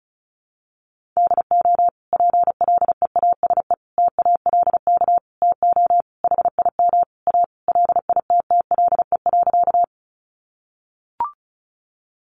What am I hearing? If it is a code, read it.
Morse code, heard as a keyed tone, about 34 wpm, 700 Hz. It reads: DO PLEASE TALK TO HIM A LITTLE.
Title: Sentences from Top 1500 Words 35wpm